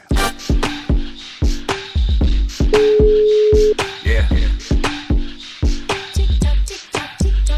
Ringback tone (Europe, 425 Hz).